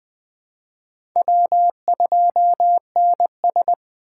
Morse code, heard as a keyed tone, about 20 wpm, 700 Hz. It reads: W2NS
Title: USA Call Signs - 4 Characters - 6x Character Spacing - 20wpm